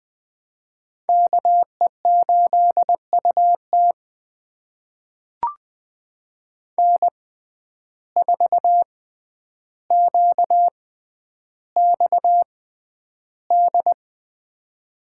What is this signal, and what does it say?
Morse code, heard as a keyed tone, about 20 wpm, spaced as for 10 wpm, 700 Hz. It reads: KE8UT N4QXD
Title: USA Call Signs - 5 Characters - 6x Character Spacing - 20wpm